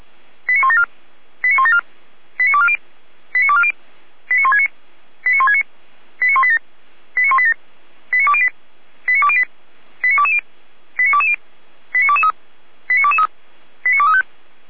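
Five-tone selective call (ZVEI), heard as quick runs of five tones, tones of about 70 ms. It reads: 89174 89174 89240 89240 89169 89169 89187 89187 89208 89208 89220 89220 89223 89223 89235
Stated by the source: ZVEI five-tone selective call received by SDR.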